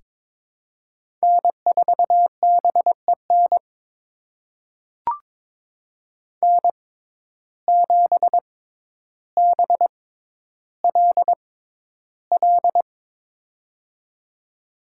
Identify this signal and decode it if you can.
Morse code, heard as a keyed tone, about 22 wpm, spaced as for 11 wpm, 700 Hz. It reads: N4BEN N7BLL
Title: USA Call Signs - 5 Characters - 6x Character Spacing - 22wpm